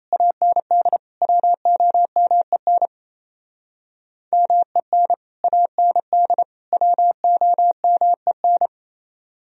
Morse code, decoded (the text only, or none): AND WOMEN MEN AND WOMEN